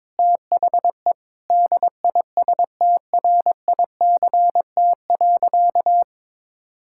Morse code, heard as a keyed tone, about 22 wpm, 700 Hz. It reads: THE DISTRICT.
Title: Sentences from Top 1600 Words 22wpm